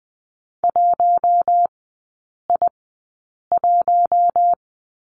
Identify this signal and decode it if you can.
Morse code, heard as a keyed tone, about 20 wpm, spaced as for 8 wpm, 700 Hz. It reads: 1I1
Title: 047 Lesson 9e - Number 4 - Three Characters 20wpm